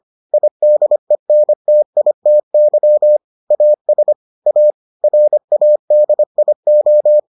Morse code, heard as a keyed tone, about 25 wpm, 600 Hz. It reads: IDENTITY AS A RADIO